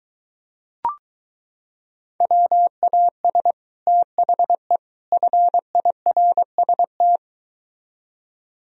Morse code, heard as a keyed tone, about 23 wpm, 700 Hz. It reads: WAS THE FIRST